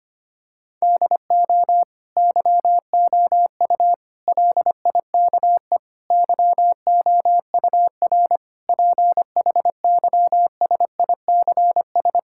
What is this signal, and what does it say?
Morse code, heard as a keyed tone, about 25 wpm, 700 Hz. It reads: DO YOU LIKE YOUR PHYSICS
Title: Sentences from Top 1000 Words - Review Part 5 - 25wpm